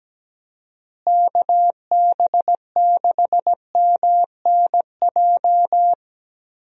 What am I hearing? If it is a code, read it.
Morse code, heard as a keyed tone, about 17 wpm, 700 Hz. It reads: KB6MNJ
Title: USA Call Signs - 6 Characters - Repeated 2x - 17wpm